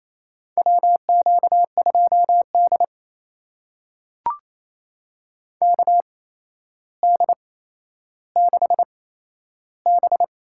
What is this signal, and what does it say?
Morse code, heard as a keyed tone, about 29 wpm, spaced as for 12 wpm, 700 Hz. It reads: WQ2D KD6B
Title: USA Call Signs - 4 Characters - 8x Character Spacing - 28wpm